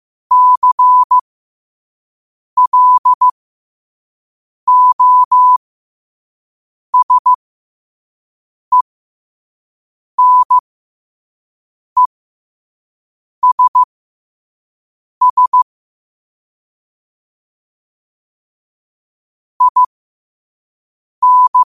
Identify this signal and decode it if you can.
Morse code, heard as a keyed tone, about 15 wpm, spaced as for 5 wpm, 1000 Hz. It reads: CLOSENESS IN
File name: qotdfiles / 5wpm